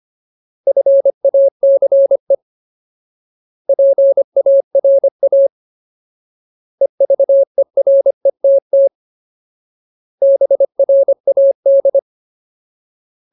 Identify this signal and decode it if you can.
Morse code, heard as a keyed tone, about 25 wpm, 550 Hz. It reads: FACE PARA EVERETT BRAD
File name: Ultimate Word List / 25WPM Ultimate